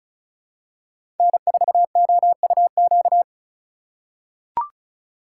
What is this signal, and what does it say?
Morse code, heard as a keyed tone, about 36 wpm, 700 Hz. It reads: N4OUQ
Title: USA Call Signs - 5 Characters - 4x Character Spacing - 35wpm